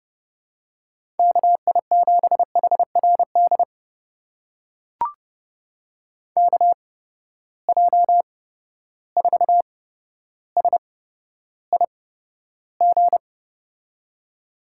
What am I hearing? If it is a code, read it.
Morse code, heard as a keyed tone, about 30 wpm, spaced as for 13 wpm, 700 Hz. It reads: KI7HRD KJ4SIG
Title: USA Call Signs - 6 Characters - 8x Character Spacing - 30wpm